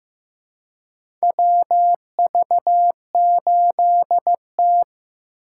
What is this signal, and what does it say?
Morse code, heard as a keyed tone, about 15 wpm, 700 Hz. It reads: WV8T
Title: USA Call Signs - 3x Character Spacing - 15wpm